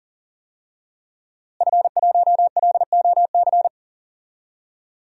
Morse code, heard as a keyed tone, about 39 wpm, 700 Hz. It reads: F1LOC
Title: DXCC-100 Call Signs - 5 Characters - Repeated 3x - 40wpm